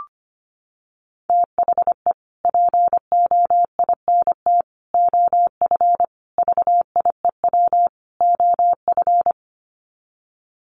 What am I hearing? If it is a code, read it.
Morse code, heard as a keyed tone, about 25 wpm, 700 Hz. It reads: THE POINT OF VIEW OF
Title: Sets of 5 Words 25wpm